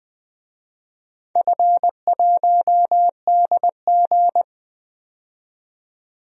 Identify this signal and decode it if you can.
Morse code, heard as a keyed tone, about 21 wpm, 700 Hz. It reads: F1DG
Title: DXCC-100 Call Signs - 4 Characters - 20wpm